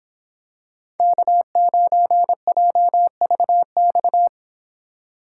Morse code, heard as a keyed tone, about 26 wpm, 700 Hz. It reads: K9JVX